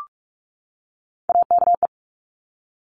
Morse code, heard as a keyed tone, about 46 wpm, 700 Hz. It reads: AKE
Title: Top Three Letter Combinations in Words 45wpm